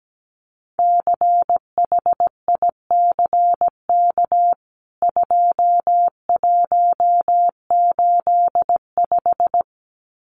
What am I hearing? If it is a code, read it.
Morse code, heard as a keyed tone, about 17 wpm, 700 Hz. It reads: CHICK 2185
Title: CWT Exchange 17wpm